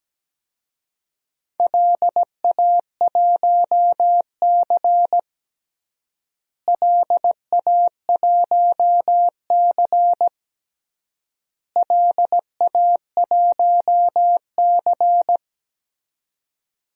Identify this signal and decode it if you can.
Morse code, heard as a keyed tone, about 17 wpm, 700 Hz. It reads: LA1C LA1C LA1C
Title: DXCC-100 Call Signs - 4 Characters - Repeated 3x - 17wpm